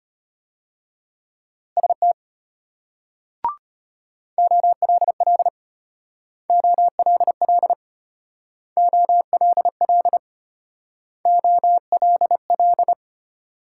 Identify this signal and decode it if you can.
Morse code, heard as a keyed tone, about 34 wpm, 700 Hz. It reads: ST OLL OLL OLL OLL